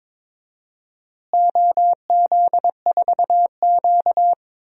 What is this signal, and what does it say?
Morse code, heard as a keyed tone, about 22 wpm, 700 Hz. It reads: OZ4Q
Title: DXCC-100 Call Signs - 4 Characters - Repeated 2x - 22wpm